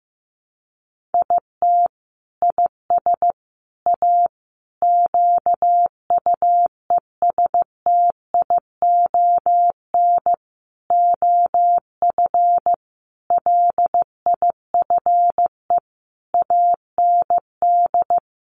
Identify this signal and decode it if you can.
Morse code, heard as a keyed tone, about 14 wpm, 700 Hz. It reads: IT IS A QUESTION OF LIFE AND
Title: Sentences from Top 1100 Words 15wpm